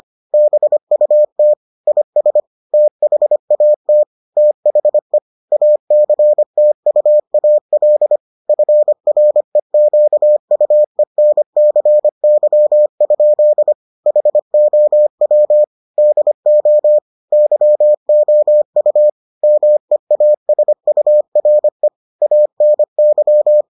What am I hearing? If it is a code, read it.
Morse code, heard as a keyed tone, about 25 wpm, 600 Hz. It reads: BUT IS THAT THE ACTUAL FREQUENCY? HOW DO YOU MEASURE ANY